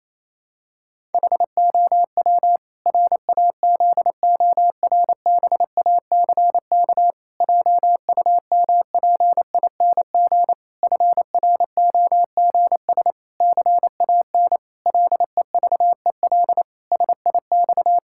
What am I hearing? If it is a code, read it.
Morse code, heard as a keyed tone, about 28 wpm, 700 Hz. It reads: HOW RAZORBACK JUMPING FROGS CAN LEVEL SIX
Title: Pangram Challenge v2 - 28wpm